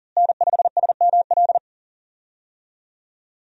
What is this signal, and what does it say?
Morse code, heard as a keyed tone, about 38 wpm, 700 Hz. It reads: N5SML
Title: USA Call Signs - 5 Characters - Repeated 2x - 40wpm